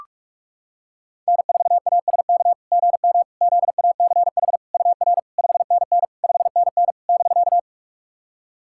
Morse code, heard as a keyed tone, about 44 wpm, 690 Hz. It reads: N4ASK GM ZACH UR 5NN 5NN <BK>
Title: SOTA QSO Practice 45wpm